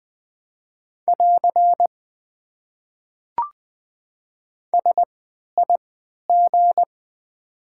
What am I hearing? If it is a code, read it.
Morse code, heard as a keyed tone, about 20 wpm, spaced as for 11 wpm, 690 Hz. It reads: <AR> SIG
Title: Top 90 QSO Elements - 3x Character Spacing - 20wpm